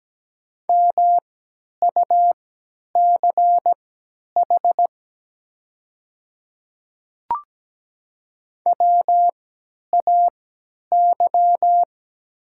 Morse code, heard as a keyed tone, about 17 wpm, spaced as for 10 wpm, 700 Hz. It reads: MUCH WAY
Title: 3x Character Spacing - Top 300 Words - 17wpm